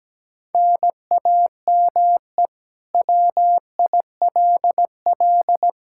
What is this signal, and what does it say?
Morse code, heard as a keyed tone, about 17 wpm, 700 Hz. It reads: NAME WILL
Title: K1USN SST QSO Practice 17wpm